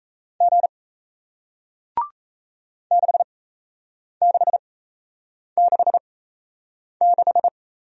Morse code, heard as a keyed tone, about 37 wpm, spaced as for 8 wpm, 700 Hz. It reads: G 6666